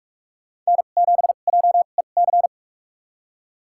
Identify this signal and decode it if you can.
Morse code, heard as a keyed tone, about 46 wpm, spaced as for 33 wpm, 690 Hz. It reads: N7JEC
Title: USA Call Signs - 5 Characters - 2x Character Spacing - 45wpm